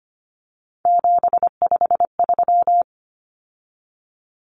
Morse code, heard as a keyed tone, about 25 wpm, 700 Hz. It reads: 753
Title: Common Cut Numbers - 3-Digit 25wpm